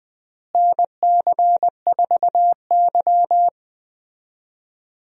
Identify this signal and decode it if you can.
Morse code, heard as a keyed tone, about 20 wpm, 700 Hz. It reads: NC4Y
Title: USA Call Signs - 4 Characters - Repeated 2x - 20wpm